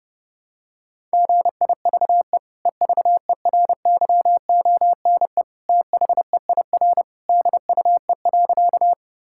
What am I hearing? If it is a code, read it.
Morse code, heard as a keyed tone, about 32 wpm, 700 Hz. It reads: GIVE EVERYONE THEIR DUE.
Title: Sentences from Top 1000 Words - Review Part 6 - 30wpm